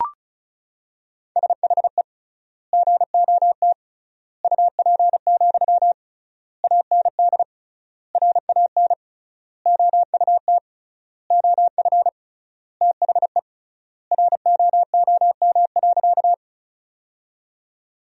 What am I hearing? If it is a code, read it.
Morse code, heard as a keyed tone, about 36 wpm, 700 Hz. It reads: SHE GOT UP, AND RAN OUT OF THE ROOM.
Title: 3x Word-Spacing - Sentences from Top 500 Words - 35wpm